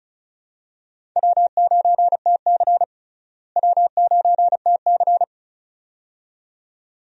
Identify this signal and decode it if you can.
Morse code, heard as a keyed tone, about 34 wpm, 700 Hz. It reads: W9TC W9TC
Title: USA Call Signs - 4 Characters - Repeated 2x - 35wpm